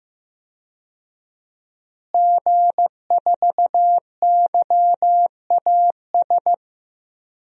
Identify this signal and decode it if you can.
Morse code, heard as a keyed tone, about 15 wpm, 700 Hz. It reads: G4YAS
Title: DXCC-100 Call Signs - 5 Characters - Repeated 2x - 15wpm